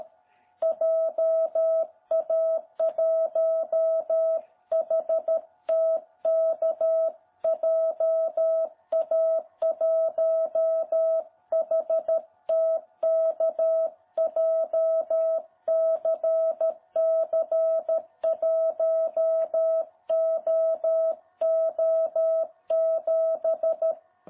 Morse code, heard as a keyed tone, about 13 wpm, 660 Hz. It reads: JA1HTK JA1HTK JCC1OO7
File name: jh8cbh / 2021cw